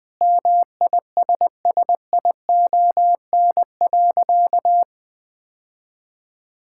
Morse code, heard as a keyed tone, about 20 wpm, 700 Hz. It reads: MISSION.
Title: Sentences from Top 1600 Words 20wpm